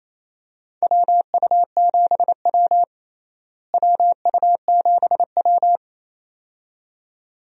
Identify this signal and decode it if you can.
Morse code, heard as a keyed tone, about 28 wpm, 700 Hz. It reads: WU7W WU7W
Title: USA Call Signs - 4 Characters - Repeated 2x - 28wpm